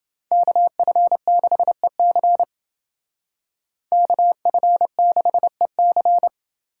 Morse code, heard as a keyed tone, about 28 wpm, 700 Hz. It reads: KF6EC KF6EC